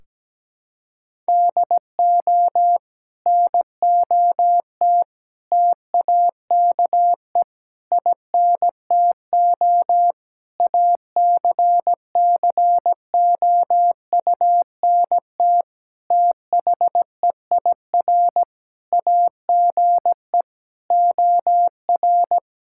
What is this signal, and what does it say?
Morse code, heard as a keyed tone, about 17 wpm, 700 Hz. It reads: DO NOT TAKE INTO ACCOUNT THEIR AGE OR
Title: Sentences from Top 1600 Words 17wpm